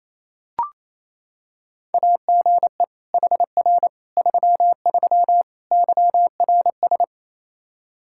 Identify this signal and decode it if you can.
Morse code, heard as a keyed tone, about 27 wpm, 700 Hz. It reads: AGE HR 33 YRS
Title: K1USN SST QSO Practice 28wpm